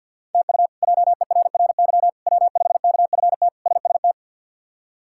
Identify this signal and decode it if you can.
Morse code, heard as a keyed tone, about 48 wpm, 700 Hz. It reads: TU JERRY W5KFT SST
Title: K1USN SST QSO Practice - 50wpm